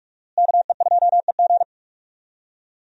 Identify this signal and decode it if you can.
Morse code, heard as a keyed tone, about 43 wpm, 700 Hz. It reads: KE2EG
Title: USA Call Signs - 5 Characters - Repeated 2x - 45wpm